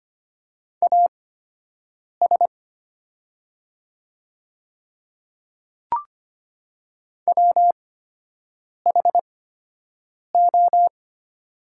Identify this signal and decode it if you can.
Morse code, heard as a keyed tone, about 25 wpm, spaced as for 7 wpm, 700 Hz. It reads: AS WHO
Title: 8x Character Spacing - Top 100 Words - 25wpm